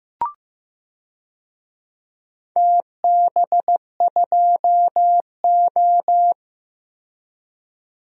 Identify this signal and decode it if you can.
Morse code, heard as a keyed tone, about 15 wpm, 700 Hz. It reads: TB2O
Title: DXCC-100 Call Signs - 4 Characters - 15wpm